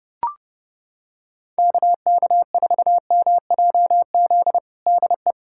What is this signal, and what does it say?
Morse code, heard as a keyed tone, about 30 wpm, 700 Hz. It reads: KK4MJZ DE
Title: USA Callsign DE Callsign - 6 Characters - 30wpm